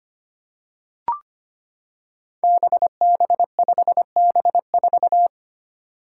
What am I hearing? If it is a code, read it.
Morse code, heard as a keyed tone, about 25 wpm, 700 Hz. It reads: BB5B4
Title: Five Challenging Letter-Number Combinations 25wpm